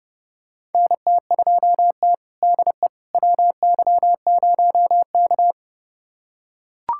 Morse code, heard as a keyed tone, about 30 wpm, 700 Hz. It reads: NT2T DE WY0K
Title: USA Callsign DE Callsign - 4 Characters - 30wpm